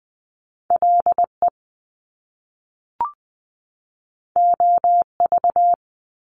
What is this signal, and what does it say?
Morse code, heard as a keyed tone, about 20 wpm, 700 Hz. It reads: LE OV
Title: Top Two Letter Combinations in Words 20wpm